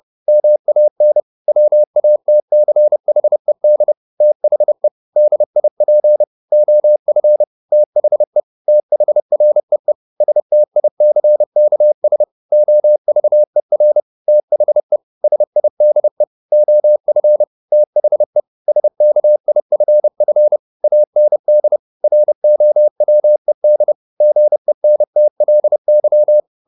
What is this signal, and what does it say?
Morse code, heard as a keyed tone, about 30 wpm, 600 Hz. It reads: MAN WATCHED THE DIP OF THE THREE STICKS OVER THE SIDE OF THE SKIFF AND ROWED GENTLY
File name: the old man and the sea / the old man and the sea 1 30wpm